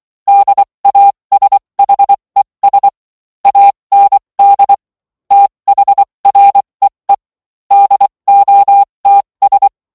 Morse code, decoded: DASHES AND THREE DOTS